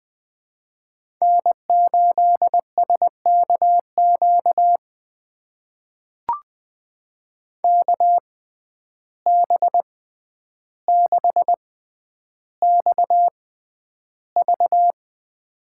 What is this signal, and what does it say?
Morse code, heard as a keyed tone, about 21 wpm, spaced as for 10 wpm, 700 Hz. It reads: N8SKQ KB6XV
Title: USA Call Signs - 5 Characters - 6x Character Spacing - 20wpm